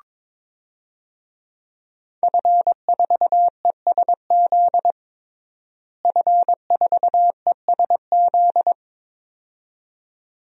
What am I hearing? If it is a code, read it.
Morse code, heard as a keyed tone, about 21 wpm, 700 Hz. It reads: F4ESZ F4ESZ
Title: DXCC-100 Call Signs - 5 Characters - Repeated 2x - 22wpm